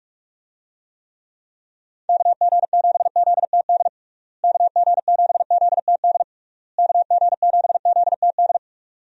Morse code, heard as a keyed tone, about 46 wpm, 700 Hz. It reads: KG7ZTD KG7ZTD KG7ZTD
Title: USA Call Signs - 6 Characters - Repeated 3x - 45wpm